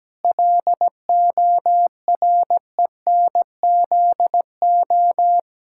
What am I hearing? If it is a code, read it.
Morse code, heard as a keyed tone, about 17 wpm, 700 Hz. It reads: LORENZO
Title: Pangram Challenge v2 - 17wpm